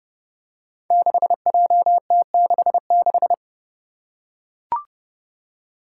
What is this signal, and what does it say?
Morse code, heard as a keyed tone, about 30 wpm, 700 Hz. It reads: 6JT66
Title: Five Challenging Letter-Number Combinations 30wpm